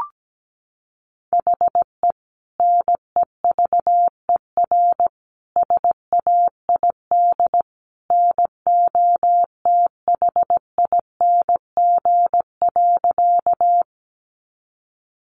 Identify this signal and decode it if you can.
Morse code, heard as a keyed tone, about 17 wpm, 700 Hz. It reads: HE NEVER SAID NOTHING.
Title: Sentences from Top 500 Words 17wpm